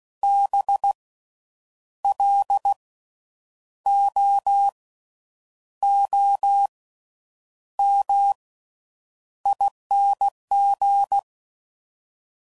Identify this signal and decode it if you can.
Morse code, heard as a keyed tone, about 16 wpm, spaced as for 8 wpm, 800 Hz. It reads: BLOOMING
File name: Slow With Faster Ings